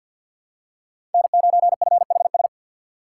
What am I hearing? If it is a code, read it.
Morse code, heard as a keyed tone, about 46 wpm, 700 Hz. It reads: N9FHS
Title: USA Call Signs - 5 Characters - 6x Character Spacing - 50wpm